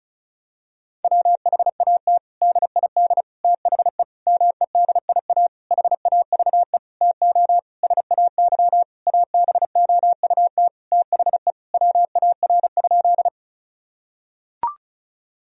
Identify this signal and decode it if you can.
Morse code, heard as a keyed tone, about 34 wpm, 700 Hz. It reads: WHAT DID THE MEDIA HAVE TO SAY ABOUT THE WAR?
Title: Sentences from Top 1000 Words - Review Part 4 - 35wpm